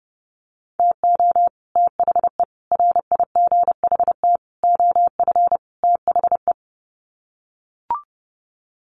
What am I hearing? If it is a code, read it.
Morse code, heard as a keyed tone, about 30 wpm, 700 Hz. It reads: TO THE RIGHT OF THE
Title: Sets of 5 Words Encore 30wpm